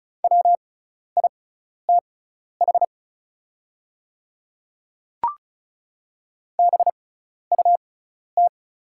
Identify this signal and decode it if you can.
Morse code, heard as a keyed tone, about 36 wpm, spaced as for 12 wpm, 690 Hz. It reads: WITH BUT
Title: 6x Character Spacing - Top 200 Words - 35wpm